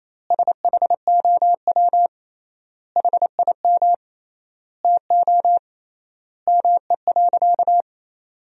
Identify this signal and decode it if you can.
Morse code, heard as a keyed tone, about 28 wpm, 700 Hz. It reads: SHOW HIM TO ME.
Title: 3x Word-Spacing - Sentences from Top 200 Words - 28wpm